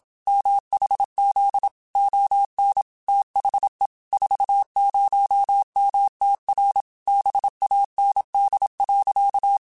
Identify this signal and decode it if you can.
Morse code, heard as a keyed tone, about 27 wpm, 800 Hz. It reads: MHZ ON THE 40MTR BAND.